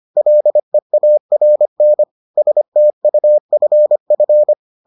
Morse code, heard as a keyed tone, about 25 wpm, 600 Hz. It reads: LEARN STUFF